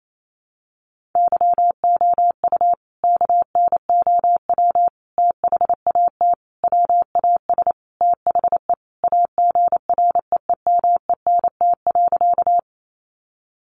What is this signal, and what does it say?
Morse code, heard as a keyed tone, about 28 wpm, 700 Hz. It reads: YOU KNOW THAT WAS THE AGREEMENT.
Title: Sentences from Top 1500 Words 28wpm